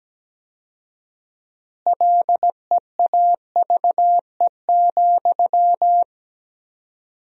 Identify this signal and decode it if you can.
Morse code, heard as a keyed tone, about 17 wpm, 700 Hz. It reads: LEAVE,
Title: 4x Word-Spacing - Sentences from Top 400 Words - 17wpm